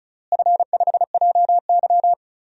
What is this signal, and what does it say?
Morse code, heard as a keyed tone, about 36 wpm, 700 Hz. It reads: F5JY
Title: DXCC-100 Call Signs - 4 Characters - Repeated 2x - 35wpm